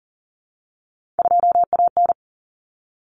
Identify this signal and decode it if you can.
Morse code, heard as a keyed tone, about 40 wpm, 700 Hz. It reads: 2AN
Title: Common Cut Numbers - 3-Digit 40wpm